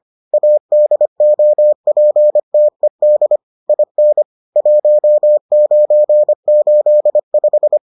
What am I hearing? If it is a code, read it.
Morse code, heard as a keyed tone, about 25 wpm, 600 Hz. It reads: ADOPTED IN 1985